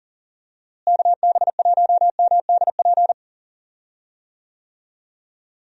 Morse code, heard as a keyed tone, about 40 wpm, 700 Hz. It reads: KB1MDP
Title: USA Call Signs - 6 Characters - Repeated 2x - 40wpm